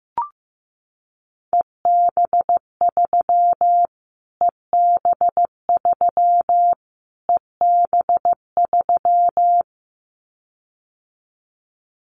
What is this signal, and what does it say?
Morse code, heard as a keyed tone, about 15 wpm, 700 Hz. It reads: EB3 EB3 EB3
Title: Callsign Prefixes for Top 25 DXCC - Repeated 3x - 15wpm